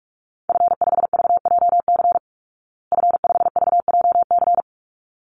Morse code, heard as a keyed tone, about 44 wpm, 700 Hz. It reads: F5VJC F5VJC
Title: Call Signs Repeated 3x - 45wpm